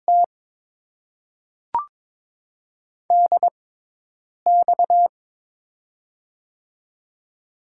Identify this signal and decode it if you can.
Morse code, heard as a keyed tone, about 23 wpm, spaced as for 8 wpm, 700 Hz. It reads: T DX
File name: Top 90 QSO Elements - 6x Character Spacing - 22wpm